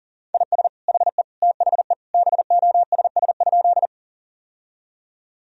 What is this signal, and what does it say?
Morse code, heard as a keyed tone, about 40 wpm, 690 Hz. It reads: IS HE THE BOSS?
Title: Sentences from Top 1000 Words - Review Part 4 - 40wpm